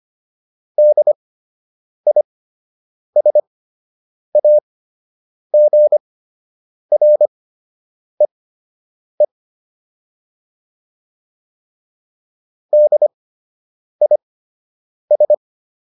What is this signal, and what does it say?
Morse code, heard as a keyed tone, about 25 wpm, spaced as for 8 wpm, 600 Hz. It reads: DISAGREE DIS